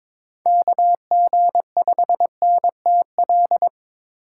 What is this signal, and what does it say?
Morse code, heard as a keyed tone, about 22 wpm, 700 Hz. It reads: KG5NTL
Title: USA Call Signs - 6 Characters - 2x Character Spacing - 22wpm